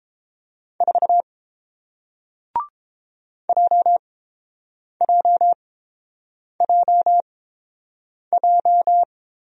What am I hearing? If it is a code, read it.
Morse code, heard as a keyed tone, about 30 wpm, spaced as for 7 wpm, 700 Hz. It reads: 4 JJJJ